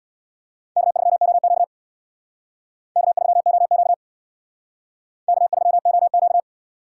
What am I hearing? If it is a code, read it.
Morse code, heard as a keyed tone, about 62 wpm, 700 Hz. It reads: K3OQ K3OQ K3OQ
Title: Speed Racing - USA 1x2 Call Signs 50wpm